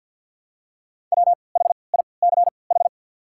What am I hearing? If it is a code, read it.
Morse code, heard as a keyed tone, about 48 wpm, spaced as for 28 wpm, 690 Hz. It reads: WHICH
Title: 3x Character Spacing - Top 300 Words - 50wpm